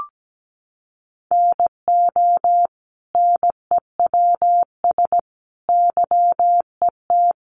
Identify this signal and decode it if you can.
Morse code, heard as a keyed tone, about 17 wpm, 700 Hz. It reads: NO NEWS YET
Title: Sentences from Top 800 Words 17wpm